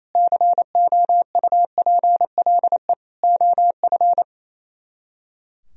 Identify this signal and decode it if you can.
Morse code, heard as a keyed tone, about 35 wpm, 700 Hz. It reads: COUPLE OF